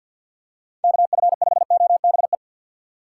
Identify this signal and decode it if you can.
Morse code, heard as a keyed tone, about 47 wpm, 700 Hz. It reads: KF5OBE
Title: USA Call Signs - 6x Character Spacing - 50wpm